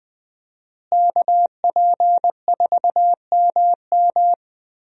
Morse code, heard as a keyed tone, about 20 wpm, 700 Hz. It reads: KP4MM
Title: USA Call Signs - 5 Characters - Repeated 2x - 20wpm